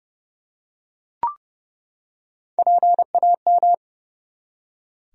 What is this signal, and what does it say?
Morse code, heard as a keyed tone, about 30 wpm, 700 Hz. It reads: PAM
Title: USA Common Names - Max 4 Letters - 30wpm